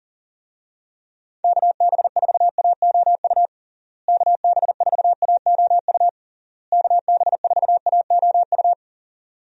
Morse code, heard as a keyed tone, about 40 wpm, 700 Hz. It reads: KB4AOU KB4AOU KB4AOU
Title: USA Call Signs - 6 Characters - Repeated 3x - 40wpm